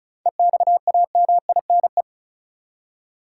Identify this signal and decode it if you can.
Morse code, heard as a keyed tone, about 34 wpm, 700 Hz. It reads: EXAMINE
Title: Seven Letter Words - 35wpm